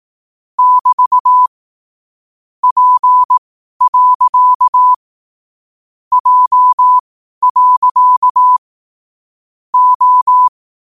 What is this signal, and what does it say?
Morse code, heard as a keyed tone, about 18 wpm, spaced as for 12 wpm, 1000 Hz. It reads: <BT> P. J. O
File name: qotdfiles / 12wpm